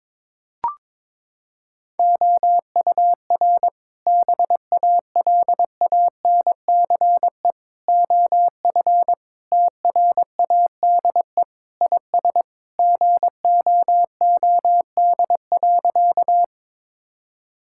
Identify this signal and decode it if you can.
Morse code, heard as a keyed tone, about 22 wpm, 700 Hz. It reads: OUR BALANCE OF TRADE IS GOOD.
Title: Sentences from Top 1000 Words - Review Part 6 - 22wpm